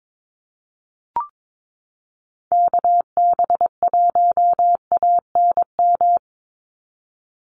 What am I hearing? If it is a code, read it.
Morse code, heard as a keyed tone, about 22 wpm, 700 Hz. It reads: KB1ANM
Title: USA Call Signs 22wpm